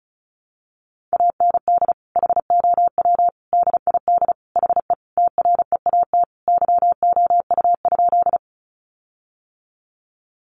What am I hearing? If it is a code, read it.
Morse code, heard as a keyed tone, about 34 wpm, 700 Hz. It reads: AND HOW DID HE TREAT YOU?
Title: Sentences from Top 1300 Words 35wpm